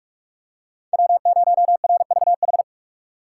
Morse code, heard as a keyed tone, about 42 wpm, 700 Hz. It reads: W0RVH
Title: USA Call Signs - 5 Characters - 2x Character Spacing - 45wpm